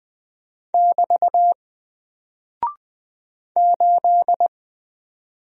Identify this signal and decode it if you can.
Morse code, heard as a keyed tone, about 20 wpm, 700 Hz. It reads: <BT> 8
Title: Single Character (Letter:Number:Punctuation:Prosign) Alt 20wpm